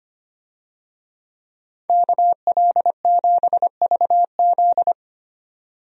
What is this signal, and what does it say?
Morse code, heard as a keyed tone, about 25 wpm, 700 Hz. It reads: KL7VZ
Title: DXCC-100 Call Signs - 5 Characters - 25wpm